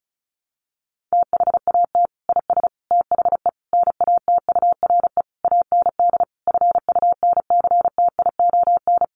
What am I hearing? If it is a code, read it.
Morse code, heard as a keyed tone, about 34 wpm, 700 Hz. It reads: THAT IS THE NATURE AND FUNCTION
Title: Sentences from Top 1100 Words 35wpm